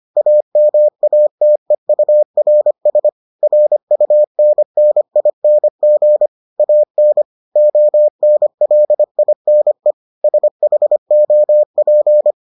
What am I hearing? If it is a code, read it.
Morse code, heard as a keyed tone, about 25 wpm, 600 Hz. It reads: AMATEURS RUNNING AN ONLINE SHOP